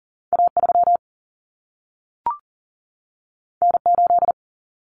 Morse code, heard as a keyed tone, about 40 wpm, 700 Hz. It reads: A3 N8
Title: Common Cut Numbers - 2-Digit 40wpm